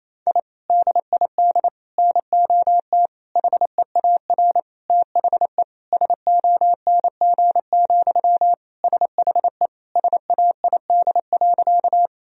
Morse code, heard as a keyed tone, about 28 wpm, 700 Hz. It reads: I DID NOT HEAR THE SONG, SHE SAID.
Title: Sentences from Top 1000 Words - Review Part 2 - 28wpm